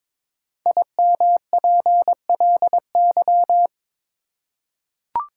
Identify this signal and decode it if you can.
Morse code, heard as a keyed tone, about 22 wpm, 700 Hz. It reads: IMPLY
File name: 5 Letter Words - First 3 Letters Given - 22wpm